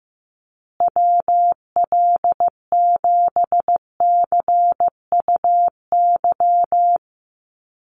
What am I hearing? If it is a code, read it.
Morse code, heard as a keyed tone, about 15 wpm, 700 Hz. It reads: WL7CUY
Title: USA Call Signs Repeated 2x 15wpm